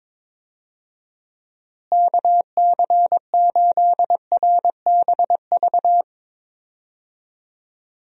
Morse code, heard as a keyed tone, about 22 wpm, 700 Hz. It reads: KC8RBV
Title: USA Call Signs - 6 Characters - Repeated 2x - 22wpm